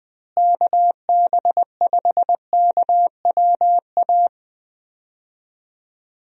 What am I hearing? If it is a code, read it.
Morse code, heard as a keyed tone, about 20 wpm, 700 Hz. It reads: KB5KWA
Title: USA Call Signs - 6 Characters - Repeated 3x - 20wpm